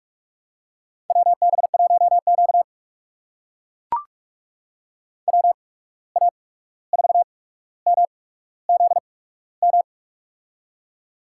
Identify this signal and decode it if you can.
Morse code, heard as a keyed tone, about 44 wpm, spaced as for 17 wpm, 700 Hz. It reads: WB1Q WA4MZM
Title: USA Call Signs - 8x Character Spacing - 45wpm